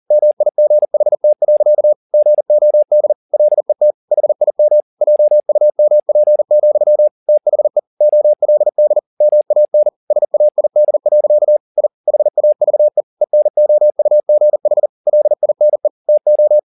Morse code, heard as a keyed tone, about 40 wpm, 600 Hz. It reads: MIGHT. GOD LET HIM JUMP, THE OLD MAN SAID. I HAVE ENOUGH LINE TO